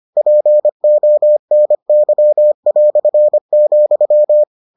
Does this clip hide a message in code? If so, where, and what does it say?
Yes: Morse code from 0.2 to 4.4 s, heard as a keyed tone, about 25 wpm, 600 Hz. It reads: PONY",